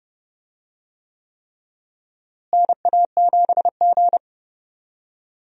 Morse code, heard as a keyed tone, about 32 wpm, 700 Hz. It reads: NA7G
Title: USA Call Signs - 4 Characters - Repeated 3x - 30wpm